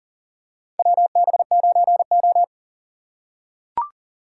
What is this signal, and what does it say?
Morse code, heard as a keyed tone, about 38 wpm, 700 Hz. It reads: WB9O
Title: USA Call Signs - 3x Character Spacing - 40wpm